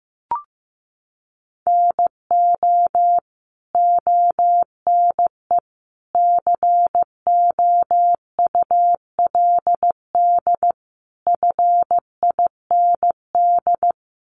Morse code, heard as a keyed tone, about 15 wpm, 700 Hz. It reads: NO ONE COULD FIND
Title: Sentences from Top 1100 Words 15wpm